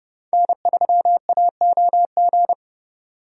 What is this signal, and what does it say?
Morse code, heard as a keyed tone, about 30 wpm, 700 Hz. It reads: N3AOG
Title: USA Call Signs - 5 Characters - Repeated 2x - 30wpm